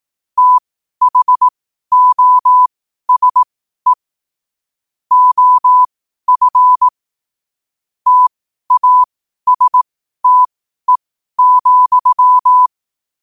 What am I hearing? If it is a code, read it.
Morse code, heard as a keyed tone, about 18 wpm, spaced as for 12 wpm, 1000 Hz. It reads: THOSE OF TASTE,
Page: qotd-at-12-wpm